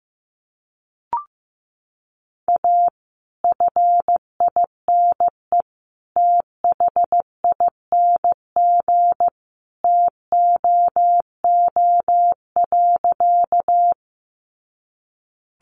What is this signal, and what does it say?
Morse code, heard as a keyed tone, about 15 wpm, 700 Hz. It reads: A FINE THING TOO.
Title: Sentences from Top 500 Words 15wpm